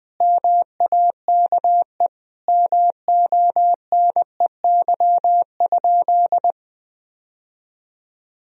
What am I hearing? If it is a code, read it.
Morse code, heard as a keyed tone, about 20 wpm, 700 Hz. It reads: MAKE MONEY?
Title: Sentences from Top 1000 Words - Review Part 4 - 20wpm